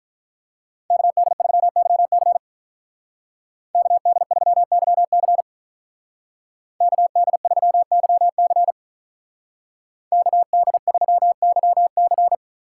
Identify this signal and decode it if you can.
Morse code, heard as a keyed tone, about 44 wpm, 700 Hz. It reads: KD3YC KD3YC KD3YC KD3YC